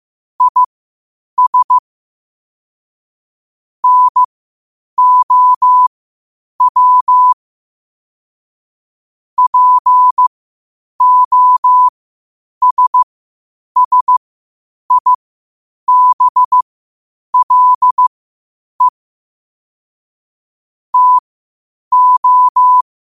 Morse code, heard as a keyed tone, about 15 wpm, spaced as for 8 wpm, 1000 Hz. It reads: IS NOW POSSIBLE TO
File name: qotdfiles / 8wpm